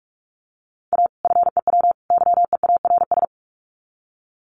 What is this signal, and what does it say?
Morse code, heard as a keyed tone, about 44 wpm, 700 Hz. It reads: A FEW YEARS
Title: Sets of 3 Words 45wpm